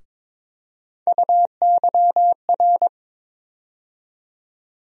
Morse code, heard as a keyed tone, about 22 wpm, 700 Hz. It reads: UYR
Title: USA Call Sign Postfixes - 3 Letters 22wpm